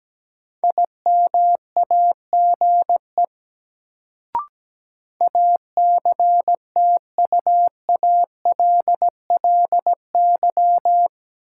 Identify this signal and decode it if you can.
Morse code, heard as a keyed tone, about 17 wpm, 700 Hz. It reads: IMAGE ACTUALLY
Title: Sentences from Top 700 Words - New Word Drill - 17wpm